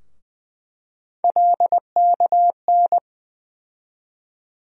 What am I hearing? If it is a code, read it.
Morse code, heard as a keyed tone, about 20 wpm, 700 Hz. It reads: LKN
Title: USA Call Sign Postfixes - 3 Letters 20wpm